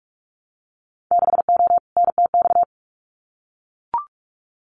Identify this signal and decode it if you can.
Morse code, heard as a keyed tone, about 44 wpm, 700 Hz. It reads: 6O NTX
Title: Winter Field Day Exchanges Encore 45wpm